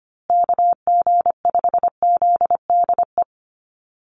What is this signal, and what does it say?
Morse code, heard as a keyed tone, about 25 wpm, 700 Hz. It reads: KG5ZDE
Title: USA Call Signs Repeated 3x 25wpm